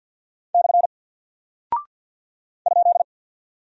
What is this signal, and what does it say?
Morse code, heard as a keyed tone, about 48 wpm, 700 Hz. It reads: / ?
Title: Single Character (Letter:Number:Punctuation:Prosign) Alt 50wpm